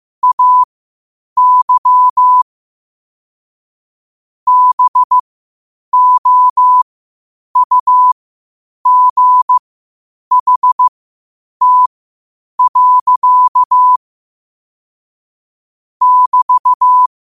Morse code, heard as a keyed tone, about 15 wpm, spaced as for 8 wpm, 1000 Hz. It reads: AY BOUGHT. <BT>
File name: qotdfiles / 8wpm